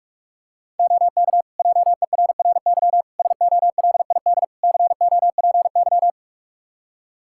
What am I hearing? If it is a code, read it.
Morse code, heard as a keyed tone, about 46 wpm, 700 Hz. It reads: OK JERRY SOLID COPY
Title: Practice QSO - Exchange 3 - More Info - 45wpm